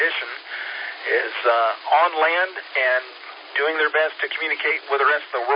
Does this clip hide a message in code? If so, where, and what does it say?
Yes: Emergency Alert System (SAME) tones (header bursts) from 0.2 to 4.8 s, heard as harsh buzzing bursts, with speech.